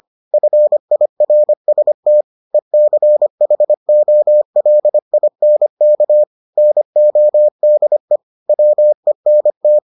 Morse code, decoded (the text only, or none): FIRST ECHOLINK NODE WENT